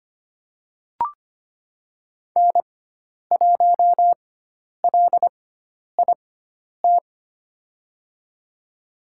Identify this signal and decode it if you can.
Morse code, heard as a keyed tone, about 25 wpm, spaced as for 10 wpm, 700 Hz. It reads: N1LIT
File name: Lesson 16g - Letter c - Call Signs 25wpm